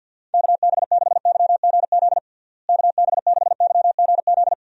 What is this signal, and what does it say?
Morse code, heard as a keyed tone, about 51 wpm, 700 Hz. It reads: KB6YGZ KB6YGZ
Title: USA Call Signs - 6 Characters - Repeated 2x - 50wpm